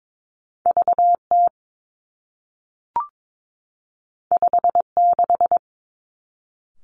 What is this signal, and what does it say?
Morse code, heard as a keyed tone, about 22 wpm, 700 Hz. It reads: V T 5 6
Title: Two Challenging Letter-Number Combinations 22wpm